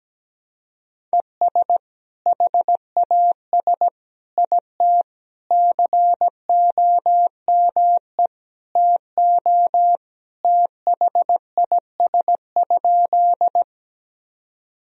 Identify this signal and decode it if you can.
Morse code, heard as a keyed tone, about 17 wpm, 700 Hz. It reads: ES HAS IT COME TO THIS?
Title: Sentences from Top 100 Words and Abbreviations 17wpm